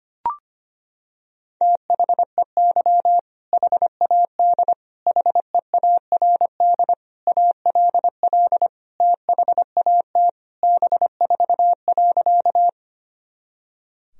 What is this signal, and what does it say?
Morse code, heard as a keyed tone, about 25 wpm, 700 Hz. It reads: THEY HAD HEARD ALL THAT B4.
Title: Sentences from Top 400 Words and Abbreviations 25wpm